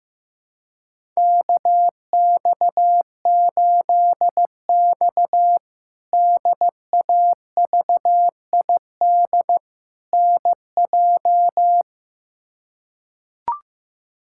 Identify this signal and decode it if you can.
Morse code, heard as a keyed tone, about 15 wpm, 700 Hz. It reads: KX8X DAVID NJ
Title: K1USN SST QSO Practice - 15wpm